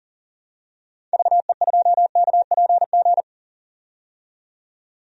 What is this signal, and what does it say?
Morse code, heard as a keyed tone, about 40 wpm, 700 Hz. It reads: VE2KPG
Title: DXCC-100 Call Signs - 6 Characters - 40wpm